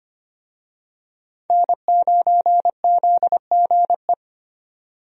Morse code, heard as a keyed tone, about 25 wpm, 700 Hz. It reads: N9ZGE
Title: USA Call Signs - 5 Characters - 25wpm